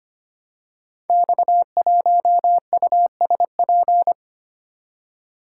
Morse code, heard as a keyed tone, about 25 wpm, 700 Hz. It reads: X1USP
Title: Code Groups - 5 Letter:Numbers - 25wpm